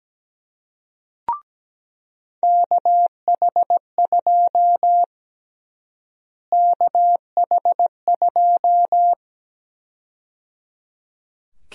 Morse code, heard as a keyed tone, about 17 wpm, 700 Hz. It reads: KH2 KH2
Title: USA Call Sign Prefixes - Repeated 2x - 17wpm